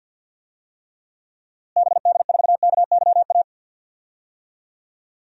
Morse code, heard as a keyed tone, about 47 wpm, 700 Hz. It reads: BD4KYA
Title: DXCC-100 Call Signs - 6 Characters - Repeated 2x - 50wpm